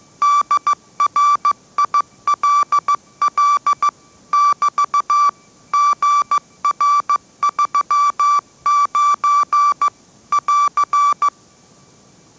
Morse code, heard as a keyed tone, about 15 wpm, 1160 Hz. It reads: DRILL <BT> GR39 <AR>
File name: mrx / mrx065 16wpm